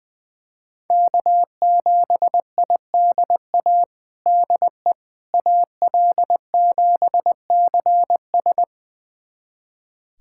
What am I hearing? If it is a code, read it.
Morse code, heard as a keyed tone, about 21 wpm, 700 Hz. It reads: K7IDA DE AL7CS
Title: USA Callsign DE Callsign - 5 Characters - 20wpm